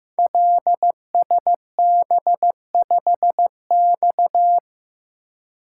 Morse code, heard as a keyed tone, about 15 wpm, 700 Hz. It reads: LSB5X
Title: Code Groups - 5 Letter:Numbers - 15wpm